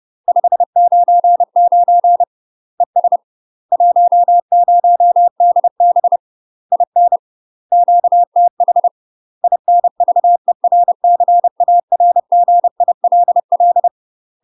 Morse code, heard as a keyed tone, about 30 wpm, 700 Hz. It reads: 599 ES 10DB IN QTH INVERCARGILL
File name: UK MASTER QSOs TRAINING / 30WPM / UK MASTER QSOs 0019